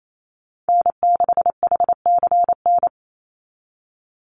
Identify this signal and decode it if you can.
Morse code, heard as a keyed tone, about 28 wpm, 700 Hz. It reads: N6HCN
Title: Call Signs Repeated 2x - 28wpm